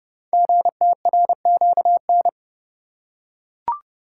Morse code, heard as a keyed tone, about 30 wpm, 700 Hz. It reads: GTRQN
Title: Code Groups - 5 Letters - 30wpm